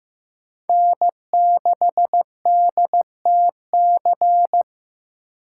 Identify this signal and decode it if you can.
Morse code, heard as a keyed tone, about 16 wpm, 700 Hz. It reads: N6DTC
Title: USA Call Signs - 5 Characters - 3x Character Spacing - 15wpm